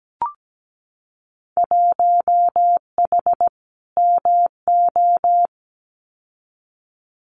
Morse code, heard as a keyed tone, about 17 wpm, 700 Hz. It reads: 1H MO
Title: Winter Field Day Exchanges 17wpm